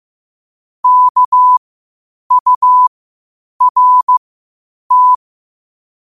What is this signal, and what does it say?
Morse code, heard as a keyed tone, about 15 wpm, spaced as for 8 wpm, 1000 Hz. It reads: KURT
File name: qotdfiles / 8wpm